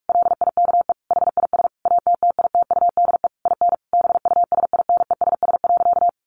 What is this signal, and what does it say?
Morse code, heard as a keyed tone, about 44 wpm, 700 Hz. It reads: LIKE HIS ATTITUDE IN BUSINESS.
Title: Sentences from Top 1300 Words 45wpm